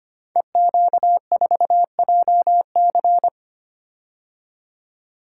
Morse code, heard as a keyed tone, about 25 wpm, 700 Hz. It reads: EQ4JC
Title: Code Groups - 5 Letter:Numbers - 25wpm